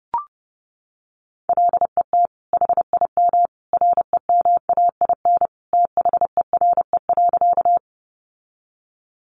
Morse code, heard as a keyed tone, about 28 wpm, 700 Hz. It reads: LET HIM REMAIN THERE.
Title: Sentences from Top 1200 Words 30wpm